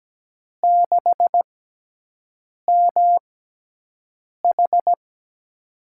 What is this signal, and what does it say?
Morse code, heard as a keyed tone, about 17 wpm, spaced as for 6 wpm, 700 Hz. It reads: 6MH
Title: USA Call Signs - 6x Character Spacing - 17wpm